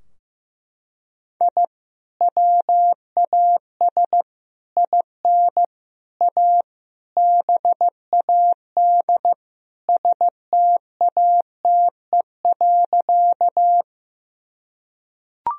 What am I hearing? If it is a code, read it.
Morse code, heard as a keyed tone, about 16 wpm, 700 Hz. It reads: I WAS IN A BAD STATE.
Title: Sentences from Top 1000 Words - Review Part 2 - 15wpm